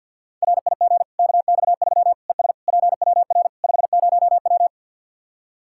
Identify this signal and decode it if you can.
Morse code, heard as a keyed tone, about 50 wpm, 700 Hz. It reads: RIG KX3 ES PWR 50W
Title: Practice QSO - Exchange 2 - Info - 50wpm